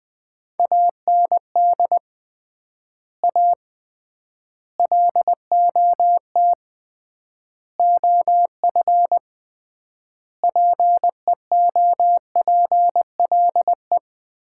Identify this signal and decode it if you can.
Morse code, heard as a keyed tone, about 20 wpm, 700 Hz. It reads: AND A LOT OF PEOPLE
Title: 3x Word-Spacing - Sets of 5 Words - 20wpm